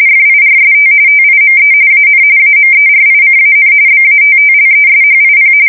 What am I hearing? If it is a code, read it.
RTTY, about 45.45 baud. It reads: FIRST TIME AN AMERICAN BATTLESHIP
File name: fsk170 / USS-New-Jersey